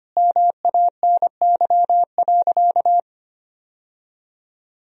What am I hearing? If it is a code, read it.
Morse code, heard as a keyed tone, about 25 wpm, 700 Hz. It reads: MANY.
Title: Sentences from Top 1000 Words - Review Part 1 - 25wpm